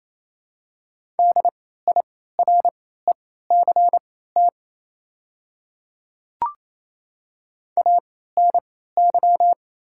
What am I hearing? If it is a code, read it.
Morse code, heard as a keyed tone, about 28 wpm, spaced as for 16 wpm, 700 Hz. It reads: DIRECT ANY
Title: 3x Character Spacing - Top 500 Words - 28wpm